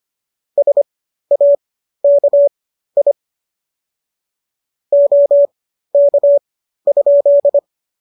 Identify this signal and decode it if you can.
Morse code, heard as a keyed tone, about 25 wpm, spaced as for 13 wpm, 570 Hz. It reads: SAKI OK?